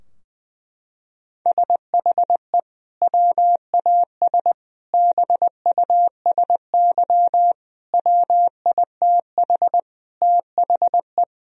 Morse code, heard as a keyed tone, about 21 wpm, 700 Hz. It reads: SHE WAS BUSY WITH THE
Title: Sentences from Top 1000 Words - Review Part 2 - 20wpm